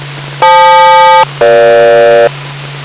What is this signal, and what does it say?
Aircraft SELCAL, heard as two pairs of tones. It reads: FL-EG